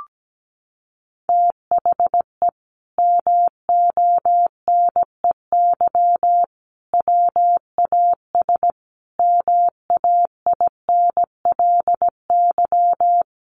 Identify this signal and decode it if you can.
Morse code, heard as a keyed tone, about 17 wpm, 700 Hz. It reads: THE MONEY WAS MAINLY
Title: Sentences from Top 1400 Words 17wpm